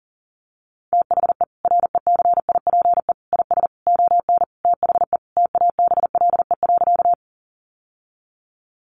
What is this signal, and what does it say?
Morse code, heard as a keyed tone, about 40 wpm, 700 Hz. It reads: THE RECIPE IS ON THE TABLE.
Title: Sentences from Top 1200 Words 40wpm